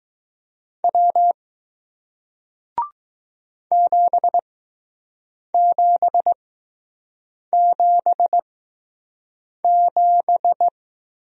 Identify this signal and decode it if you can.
Morse code, heard as a keyed tone, about 20 wpm, spaced as for 6 wpm, 700 Hz. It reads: W 7777